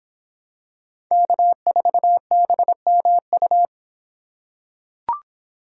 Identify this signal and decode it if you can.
Morse code, heard as a keyed tone, about 26 wpm, 700 Hz. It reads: K4BMU